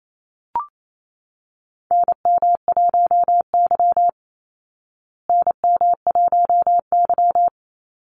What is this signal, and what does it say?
Morse code, heard as a keyed tone, about 28 wpm, 700 Hz. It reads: NM1Y NM1Y
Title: Call Signs Repeated 2x - 28wpm